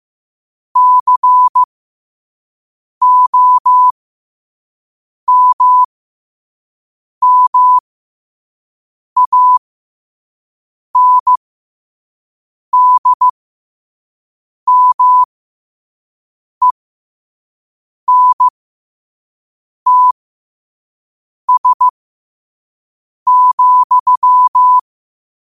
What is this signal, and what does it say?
Morse code, heard as a keyed tone, about 15 wpm, spaced as for 5 wpm, 1000 Hz. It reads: COMMANDMENTS,
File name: qotdfiles / 5wpm